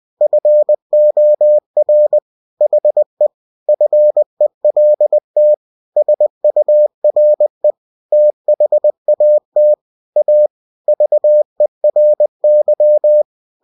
Morse code, heard as a keyed tone, about 20 wpm, 600 Hz. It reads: FOR HE FELT SURE THAT A VERY
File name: cw-books / winnie-the-pooh / winnie-20wpm-page-0036